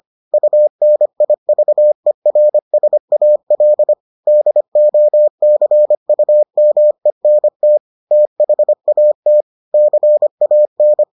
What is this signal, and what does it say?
Morse code, heard as a keyed tone, about 25 wpm, 600 Hz. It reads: UNIVERSAL DOCUMENT THAT CAN